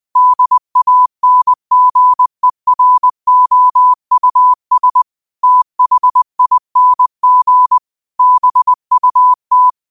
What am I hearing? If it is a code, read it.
Morse code, heard as a keyed tone, about 21 wpm, 1000 Hz. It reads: DANGEROUS THING BUT
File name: qotdfiles / 20wpm